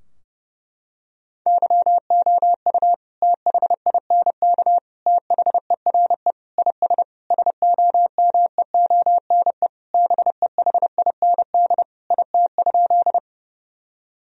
Morse code, heard as a keyed tone, about 30 wpm, 700 Hz. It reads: YOU THINK THERE IS SOMEONE BEHIND IT?
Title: Sentences from Top 1000 Words - Review Part 4 - 30wpm